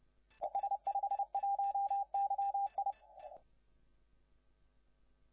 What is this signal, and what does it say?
Morse code, heard as a keyed tone, about 30 wpm, 800 Hz, with radio noise. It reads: S51YI
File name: la4c / LA4C de YT7KM CQ160CW CW 1,8MHz